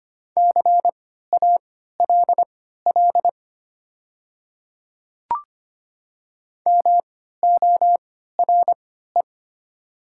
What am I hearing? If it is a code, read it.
Morse code, heard as a keyed tone, about 25 wpm, spaced as for 14 wpm, 700 Hz. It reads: CALL MORE
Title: 3x Character Spacing - Top 200 Words - 25wpm